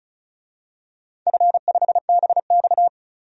Morse code, heard as a keyed tone, about 34 wpm, 700 Hz. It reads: F5BX
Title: DXCC-100 Call Signs - 4 Characters - 35wpm